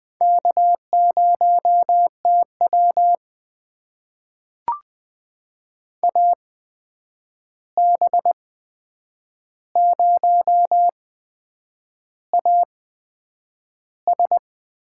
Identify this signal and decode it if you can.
Morse code, heard as a keyed tone, about 20 wpm, spaced as for 8 wpm, 700 Hz. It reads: K0TW AB0AS
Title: USA Call Signs - 8x Character Spacing - 20wpm